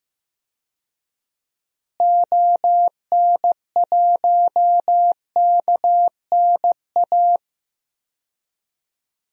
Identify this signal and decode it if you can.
Morse code, heard as a keyed tone, about 15 wpm, 700 Hz. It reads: ON1KNA
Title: DXCC-100 Call Signs - 6 Characters - Repeated 3x - 15wpm